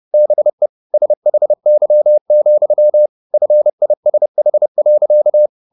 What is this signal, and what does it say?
Morse code, heard as a keyed tone, about 30 wpm, 600 Hz. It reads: BE SHY, FISH.
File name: the old man and the sea / the old man and the sea 1 30wpm